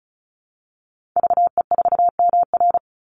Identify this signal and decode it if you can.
Morse code, heard as a keyed tone, about 34 wpm, 700 Hz. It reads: VE4MR
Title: Call Signs Repeated 3x - 35wpm